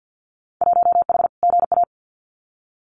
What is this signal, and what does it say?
Morse code, heard as a keyed tone, about 48 wpm, 700 Hz. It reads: 1H GA
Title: Winter Field Day Exchanges 50wpm